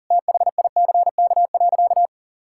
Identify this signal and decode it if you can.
Morse code, heard as a keyed tone, about 40 wpm, 700 Hz. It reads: THICK.
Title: Sentences from Top 1600 Words 40wpm